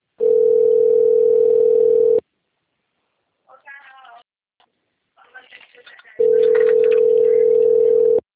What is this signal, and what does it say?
Ringback tone (North America), 440 + 480 Hz, with speech.
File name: rg-1600-unknown- - - .41672